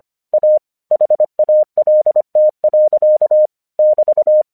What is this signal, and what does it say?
Morse code, heard as a keyed tone, about 25 wpm, 610 Hz. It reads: A HALT. <BT>
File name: 1 English 25wpm / AQC04